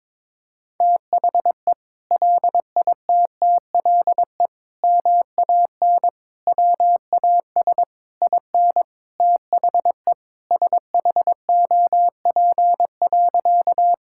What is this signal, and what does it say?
Morse code, heard as a keyed tone, about 22 wpm, 700 Hz. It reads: THE LITTLE MAN WAS IN THE SHOP.
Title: Sentences from Top 1600 Words 22wpm